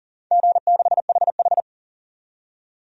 Morse code, heard as a keyed tone, about 41 wpm, 700 Hz. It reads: G6HH
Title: DXCC-100 Call Signs - 4 Characters - 40wpm